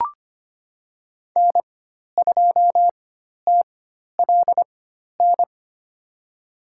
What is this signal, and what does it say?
Morse code, heard as a keyed tone, about 25 wpm, spaced as for 12 wpm, 700 Hz. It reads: N2TLN
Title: USA Call Signs - 5 Characters - 4x Character Spacing - 25wpm